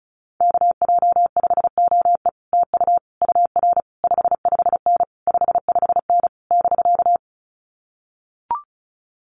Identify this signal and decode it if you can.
Morse code, heard as a keyed tone, about 36 wpm, 700 Hz. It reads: KJ5OE TU UR 55N 55N <BK>
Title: Common QSO Phrases Encore 35wpm